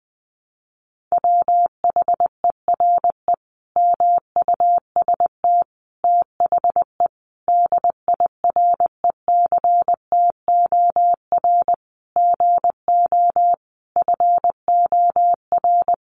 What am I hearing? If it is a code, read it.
Morse code, heard as a keyed tone, about 20 wpm, 700 Hz. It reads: WHERE MUST THE DIRECTOR GO FOR
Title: Sentences from Top 1300 Words 20wpm